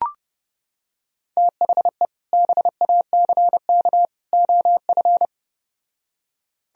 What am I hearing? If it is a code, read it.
Morse code, heard as a keyed tone, about 30 wpm, 700 Hz. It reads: THE BACK OF